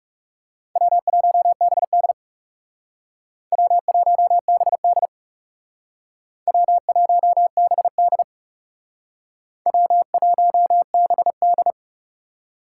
Morse code, heard as a keyed tone, about 38 wpm, 700 Hz. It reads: W1BD W1BD W1BD W1BD